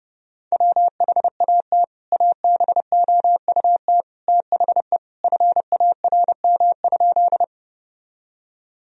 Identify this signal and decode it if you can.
Morse code, heard as a keyed tone, about 30 wpm, 700 Hz. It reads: WHAT ABOUT THE FARM?
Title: Sentences from Top 1000 Words - Review Part 2 - 30wpm